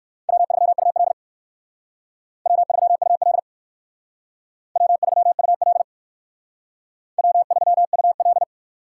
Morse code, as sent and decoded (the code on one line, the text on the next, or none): .-- ...-- ..- .-.. / .-- ...-- ..- .-.. / .-- ...-- ..- .-.. / .-- ...-- ..- .-..
W3UL W3UL W3UL W3UL